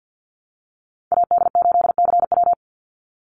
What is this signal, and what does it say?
Morse code, heard as a keyed tone, about 48 wpm, 700 Hz. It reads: AD8CW
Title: Call Signs Repeated 3x - 50wpm